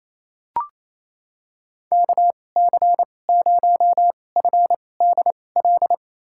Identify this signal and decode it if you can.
Morse code, heard as a keyed tone, about 28 wpm, spaced as for 20 wpm, 700 Hz. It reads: KC0FDL
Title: USA Call Signs - 6 Characters - 2x Character Spacing - 28wpm